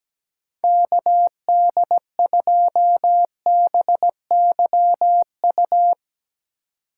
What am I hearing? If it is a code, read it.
Morse code, heard as a keyed tone, about 17 wpm, 700 Hz. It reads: KD2BYU
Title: USA Call Signs - 6 Characters - 3x Character Spacing - 17wpm